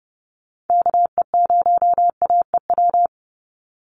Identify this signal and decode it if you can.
Morse code, heard as a keyed tone, about 29 wpm, 700 Hz. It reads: KE0AEW
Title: USA Call Signs 30wpm